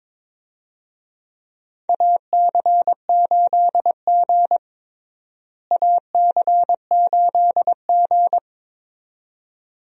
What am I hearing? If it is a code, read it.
Morse code, heard as a keyed tone, about 22 wpm, 700 Hz. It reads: AC8G AC8G
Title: USA Call Signs - 4 Characters - Repeated 2x - 22wpm